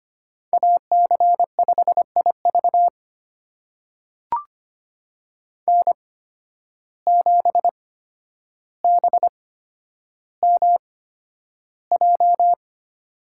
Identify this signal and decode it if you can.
Morse code, heard as a keyed tone, about 25 wpm, spaced as for 11 wpm, 700 Hz. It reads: AC5IV N7BMJ
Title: USA Call Signs - 5 Characters - 8x Character Spacing - 25wpm